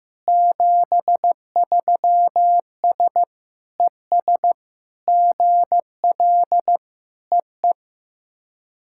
Morse code, heard as a keyed tone, about 16 wpm, 700 Hz. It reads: 73S ES GL EE
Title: SOTA QSO Practice 15wpm